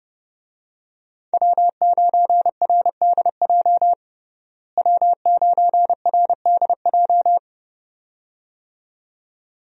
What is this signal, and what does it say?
Morse code, heard as a keyed tone, about 30 wpm, 700 Hz. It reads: W9RDJ W9RDJ
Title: USA Call Signs - 5 Characters - Repeated 2x - 30wpm